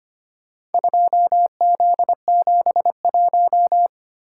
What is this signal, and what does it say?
Morse code, heard as a keyed tone, about 25 wpm, 700 Hz. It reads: 2Z71
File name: practice / 25wpm / Four Letter-Number Combinations - 25wpm